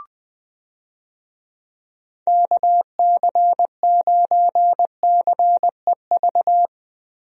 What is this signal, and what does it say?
Morse code, heard as a keyed tone, about 20 wpm, 700 Hz. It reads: KC9CEV
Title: USA Call Signs - 6 Characters - Repeated 2x - 20wpm